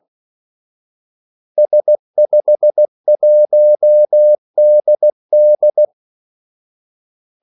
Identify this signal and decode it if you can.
Morse code, heard as a keyed tone, about 16 wpm, 600 Hz. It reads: S51DD
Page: Kaldesignaler 16 wpm